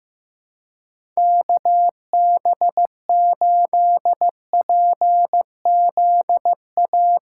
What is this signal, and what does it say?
Morse code, heard as a keyed tone, about 15 wpm, 700 Hz. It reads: KB8PZA
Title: USA Call Signs - 6 Characters - 8x Character Spacing - 15wpm